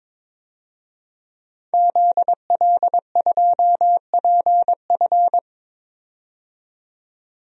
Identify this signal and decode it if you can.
Morse code, heard as a keyed tone, about 22 wpm, 700 Hz. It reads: ZL2PF
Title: DXCC-100 Call Signs - 5 Characters - 22wpm